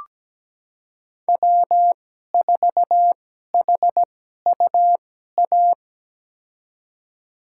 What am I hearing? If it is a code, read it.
Morse code, heard as a keyed tone, about 17 wpm, spaced as for 12 wpm, 700 Hz. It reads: W4HUA
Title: USA Call Signs - 5 Characters - 2x Character Spacing - 17wpm